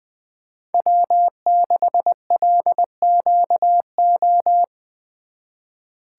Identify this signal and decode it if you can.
Morse code, heard as a keyed tone, about 23 wpm, 700 Hz. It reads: W6LQO
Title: Speed Racing - USA Call Signs 20wpm